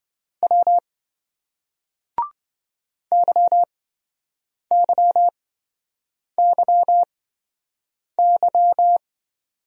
Morse code, heard as a keyed tone, about 28 wpm, spaced as for 7 wpm, 700 Hz. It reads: W YYYY